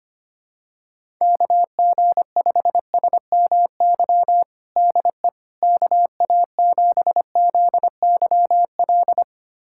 Morse code, heard as a keyed tone, about 25 wpm, 700 Hz. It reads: KG5SMY DE KA7ZYL
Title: USA Callsign DE Callsign - 6 Characters - 25wpm